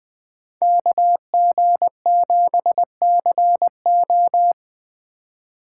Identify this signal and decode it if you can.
Morse code, heard as a keyed tone, about 20 wpm, 700 Hz. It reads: KG7CO
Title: Speed Racing - USA 2x2 Call Signs 17wpm